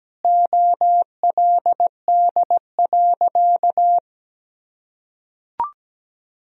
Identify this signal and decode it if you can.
Morse code, heard as a keyed tone, about 17 wpm, 700 Hz. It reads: OLD.
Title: Sentences from Top 1000 Words - Review Part 4 - 17wpm